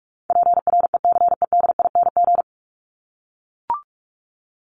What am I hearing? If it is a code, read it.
Morse code, heard as a keyed tone, about 44 wpm, 700 Hz. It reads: PRECEDING
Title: ING Suffix - Top 100 Words with Base Words 45wpm